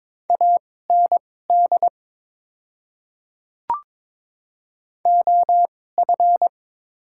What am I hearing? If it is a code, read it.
Morse code, heard as a keyed tone, about 22 wpm, spaced as for 16 wpm, 700 Hz. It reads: AND OF
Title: 2x Character Spacing - Top 100 Words - 22wpm